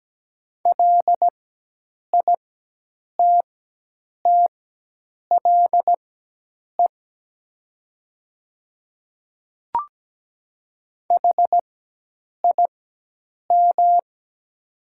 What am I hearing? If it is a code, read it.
Morse code, heard as a keyed tone, about 17 wpm, spaced as for 8 wpm, 700 Hz. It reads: LITTLE HIM
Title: 4x Character Spacing - Top 300 Words - 17wpm